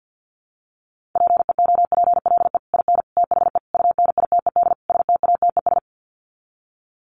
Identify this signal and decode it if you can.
Morse code, heard as a keyed tone, about 48 wpm, 700 Hz. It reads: PEOPLE IN THE UNITED STATES
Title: Sets of 5 Words Encore 50wpm